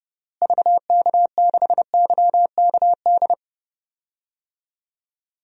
Morse code, heard as a keyed tone, about 30 wpm, 700 Hz. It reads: VK6YKD
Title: DXCC-100 Call Signs - 6 Characters - Repeated 2x - 30wpm